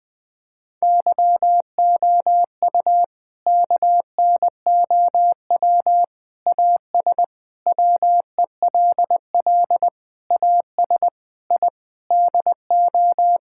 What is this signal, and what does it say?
Morse code, heard as a keyed tone, about 20 wpm, 700 Hz. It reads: YOU KNOW AS WELL AS I DO
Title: Sets of 8 Words - 20wpm